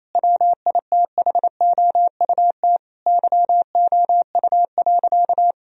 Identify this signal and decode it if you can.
Morse code, heard as a keyed tone, about 27 wpm, 700 Hz. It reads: WITHOUT YOU.
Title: Sentences from Top 1000 Words - Review Part 3 - 28wpm